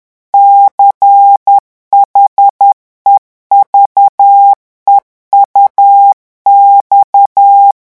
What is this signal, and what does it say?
Morse code, heard as a keyed tone, about 10 wpm, 800 Hz. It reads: CHEVEUX
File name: revision9 citations humoristiques vitesse 10 mots minute